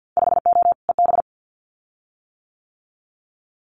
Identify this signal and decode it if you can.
Morse code, heard as a keyed tone, about 50 wpm, 700 Hz. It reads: 5O EB
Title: Winter Field Day Exchanges Encore 50wpm